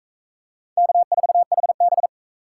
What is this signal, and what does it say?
Morse code, heard as a keyed tone, about 39 wpm, 700 Hz. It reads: K4HB